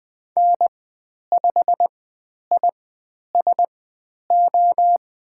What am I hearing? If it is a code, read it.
Morse code, heard as a keyed tone, about 20 wpm, spaced as for 10 wpm, 700 Hz. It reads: N5ISO
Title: 102 Lesson 17g - Letter u - Call Signs 20wpm